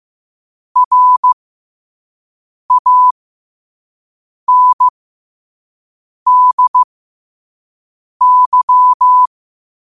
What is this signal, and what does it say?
Morse code, heard as a keyed tone, about 15 wpm, spaced as for 5 wpm, 1000 Hz. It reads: RANDY